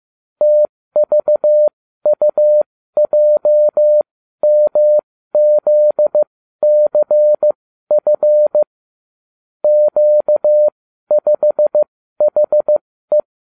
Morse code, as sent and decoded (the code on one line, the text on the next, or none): - ...- ..- .--- -- --.. -.-. ..-. / --.- ..... .... .
TVUJMZCF Q5HE